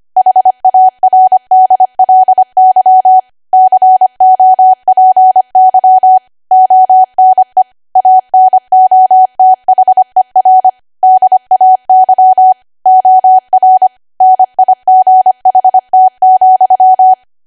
Morse code, heard as a keyed tone, about 25 wpm, 750 Hz. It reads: HARDLY COPY ONE ANOTHER DAY OR NIGHT,
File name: Archive / 25 WPM / 25WPM